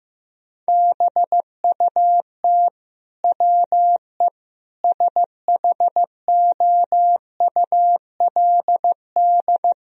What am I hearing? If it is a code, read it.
Morse code, heard as a keyed tone, about 15 wpm, 700 Hz. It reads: BUT WE SHOULD
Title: Sentences from Top 1000 Words - Review Part 4 - 15wpm